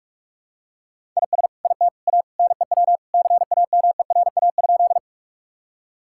Morse code, heard as a keyed tone, about 46 wpm, 700 Hz. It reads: IS IT A NEW CAMERA?
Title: Sentences from Top 1000 Words - Review Part 5 - 45wpm